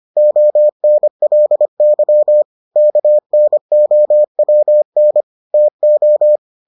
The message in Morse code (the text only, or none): ONLY KNOWN TO